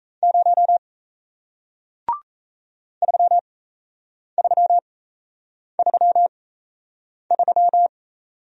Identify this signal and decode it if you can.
Morse code, heard as a keyed tone, about 37 wpm, spaced as for 8 wpm, 700 Hz. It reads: 0 3333